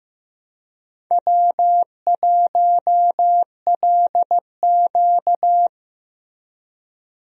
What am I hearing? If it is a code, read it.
Morse code, heard as a keyed tone, about 15 wpm, 700 Hz. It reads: W1LQ
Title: USA Call Signs - 4 Characters - Repeated 3x - 15wpm